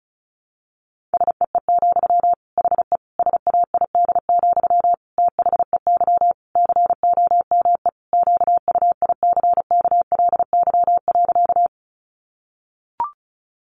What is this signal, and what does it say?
Morse code, heard as a keyed tone, about 34 wpm, 700 Hz. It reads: SEE, HE SAID, THEY COME QUICKLY.
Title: Sentences from Top 700 Words 35wpm